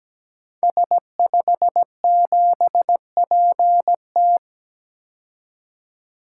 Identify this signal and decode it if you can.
Morse code, heard as a keyed tone, about 17 wpm, 700 Hz. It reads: S57PT
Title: Code Groups - 5 Letter:Numbers - 17wpm